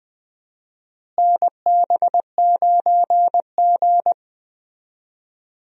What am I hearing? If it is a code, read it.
Morse code, heard as a keyed tone, about 20 wpm, 700 Hz. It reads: NB9G
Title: USA Call Signs - 4 Characters - Repeated 3x - 20wpm